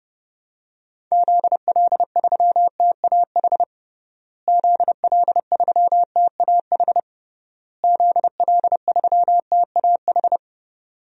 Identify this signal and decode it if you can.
Morse code, heard as a keyed tone, about 30 wpm, 700 Hz. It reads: ZL3TAH ZL3TAH ZL3TAH
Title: DXCC-100 Call Signs - 6 Characters - Repeated 3x - 30wpm